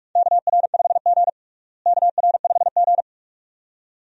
Morse code, heard as a keyed tone, about 47 wpm, 700 Hz. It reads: KR5G KR5G
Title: USA Call Signs - 4 Characters - Repeated 2x - 45wpm